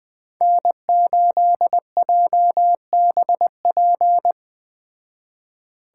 Morse code, heard as a keyed tone, about 20 wpm, 700 Hz. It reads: N8JBP